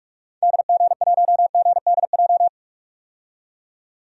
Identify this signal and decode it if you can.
Morse code, heard as a keyed tone, about 46 wpm, 700 Hz. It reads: DG1GDJ
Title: DXCC-100 Call Signs - 6 Characters - 45wpm